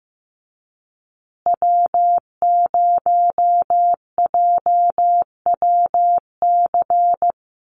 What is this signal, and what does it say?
Morse code, heard as a keyed tone, about 15 wpm, 700 Hz. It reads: W0JWC
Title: USA Call Signs Repeated 2x 15wpm